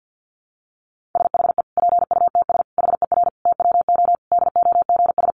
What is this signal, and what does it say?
Morse code, heard as a keyed tone, about 50 wpm, 700 Hz. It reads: SHE PUTS HER TWO DOGS
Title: Easy Phrases 50wpm